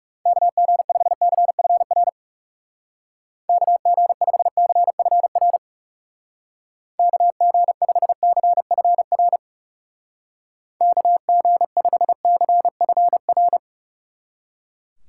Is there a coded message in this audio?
Yes: Morse code from 0.3 to 13.6 s, heard as a keyed tone, about 38 wpm, 700 Hz. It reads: KG5CFR KG5CFR KG5CFR KG5CFR